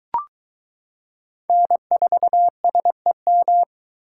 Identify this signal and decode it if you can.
Morse code, heard as a keyed tone, about 22 wpm, 700 Hz. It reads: N4SEM